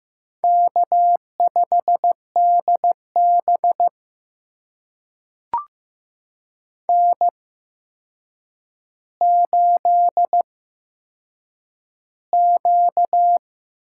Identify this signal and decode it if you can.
Morse code, heard as a keyed tone, about 15 wpm, spaced as for 7 wpm, 700 Hz. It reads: K5DB N8Q
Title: USA Call Signs - 4 Characters - 8x Character Spacing - 15wpm